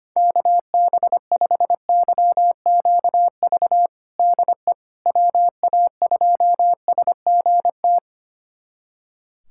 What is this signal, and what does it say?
Morse code, heard as a keyed tone, about 25 wpm, 700 Hz. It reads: KB5YQV DE WA2SGT
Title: USA Callsign DE Callsign 25wpm